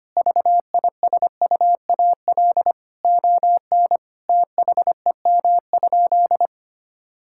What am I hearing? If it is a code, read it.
Morse code, heard as a keyed tone, about 25 wpm, 700 Hz. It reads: VISUAL ON THEM?
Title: Sentences from Top 1000 Words - Review Part 6 - 25wpm